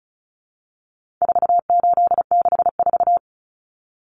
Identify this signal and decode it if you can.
Morse code, heard as a keyed tone, about 34 wpm, 700 Hz. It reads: 4864
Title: Numbers - 4-Digit 35wpm